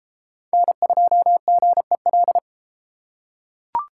Morse code, heard as a keyed tone, about 30 wpm, 700 Hz. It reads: N2GEL